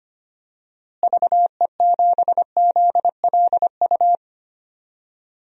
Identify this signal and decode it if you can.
Morse code, heard as a keyed tone, about 25 wpm, 700 Hz. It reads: VE7ZLU
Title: DXCC-100 Call Signs - 6 Characters - Repeated 3x - 25wpm